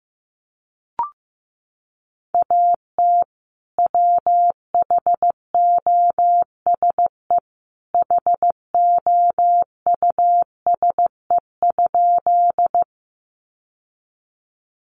Morse code, heard as a keyed tone, about 15 wpm, 700 Hz. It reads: AT WHOSE HOUSE?
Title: Sentences from Top 1200 Words 15wpm